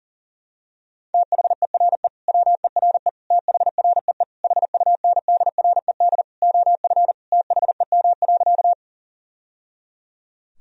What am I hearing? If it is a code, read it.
Morse code, heard as a keyed tone, about 40 wpm, 700 Hz. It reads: THERE WERE THREE HUNDRED OF THEM.
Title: Sentences from Top 1000 Words - Review Part 2 - 40wpm